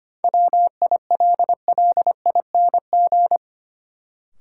Morse code, heard as a keyed tone, about 25 wpm, 700 Hz. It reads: WILLING
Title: Sentences from Top 1000 Words - New Word Drill - 25wpm